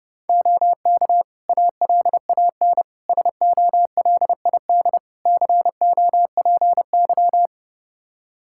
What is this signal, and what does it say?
Morse code, heard as a keyed tone, about 30 wpm, 700 Hz. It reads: OK ALAN SOLID COPY
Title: Practice QSO - Exchange 3 - More Info - 30wpm